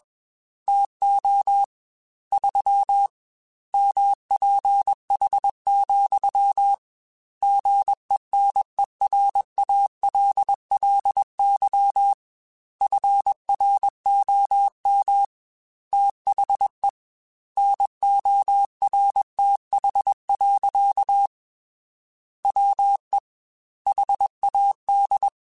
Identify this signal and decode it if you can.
Morse code, heard as a keyed tone, about 21 wpm, 800 Hz. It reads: TO 3 MPH, GENERALLY FROM THE NORTH. WE HAD
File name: Weather Sets / Weather Sets-mp3s / Weather 4 28 @20